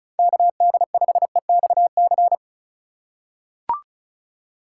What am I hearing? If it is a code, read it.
Morse code, heard as a keyed tone, about 36 wpm, 700 Hz. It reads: KD5EXC
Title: USA Call Signs - 6 Characters - 3x Character Spacing - 35wpm